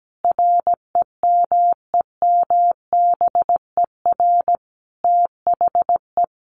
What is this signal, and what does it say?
Morse code, heard as a keyed tone, about 17 wpm, 700 Hz. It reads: REMEMBER THE
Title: Sentences from Top 400 Words 17wpm